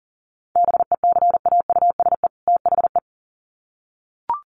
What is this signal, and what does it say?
Morse code, heard as a keyed tone, about 40 wpm, 700 Hz. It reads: BECAUSE THE
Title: Sets of 2 Words 40wpm